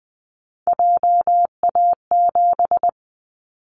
Morse code, heard as a keyed tone, about 20 wpm, 700 Hz. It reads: JA7
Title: Callsign Prefixes for Top 50 DXCC - 20wpm